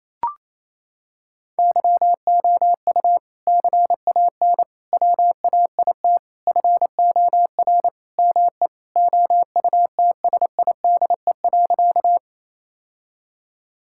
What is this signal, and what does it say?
Morse code, heard as a keyed tone, about 27 wpm, 700 Hz. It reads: YOU CAN WAIT FOR ME OUTSIDE.
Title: Sentences from Top 1000 Words - Review Part 6 - 28wpm